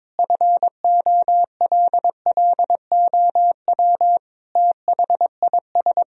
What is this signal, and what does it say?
Morse code, heard as a keyed tone, about 22 wpm, 700 Hz. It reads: FOLLOW THIS
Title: Sentences from Top 1000 Words - Review Part 1 - 22wpm